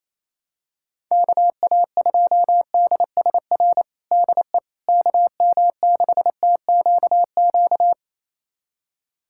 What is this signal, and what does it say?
Morse code, heard as a keyed tone, about 28 wpm, 700 Hz. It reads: KA2DSR DE KM6TQQ
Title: USA Callsign DE Callsign 28wpm